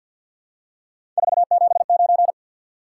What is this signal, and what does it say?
Morse code, heard as a keyed tone, about 54 wpm, 700 Hz. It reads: 479
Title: Top 180 QSO Elements 50wpm